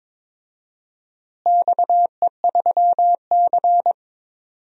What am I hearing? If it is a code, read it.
Morse code, heard as a keyed tone, about 22 wpm, 700 Hz. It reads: XE3C
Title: DXCC-100 Call Signs - 4 Characters - Repeated 3x - 22wpm